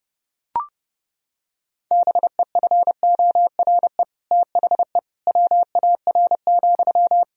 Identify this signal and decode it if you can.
Morse code, heard as a keyed tone, about 30 wpm, 700 Hz. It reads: BEFORE THE WAR,
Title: Sentences from Top 1000 Words - Review Part 2 - 30wpm